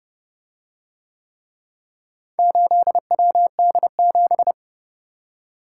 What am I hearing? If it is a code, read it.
Morse code, heard as a keyed tone, about 29 wpm, 700 Hz. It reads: 8WD7